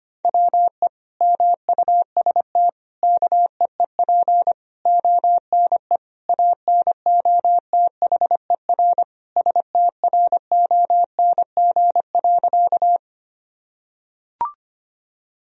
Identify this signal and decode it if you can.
Morse code, heard as a keyed tone, about 25 wpm, 700 Hz. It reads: WE MUST KEEP ONE ANOTHER STRONG.
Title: Sentences from Top 1000 Words - Review Part 3 - 25wpm